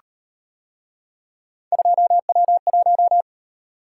Morse code, heard as a keyed tone, about 36 wpm, 700 Hz. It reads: 2W1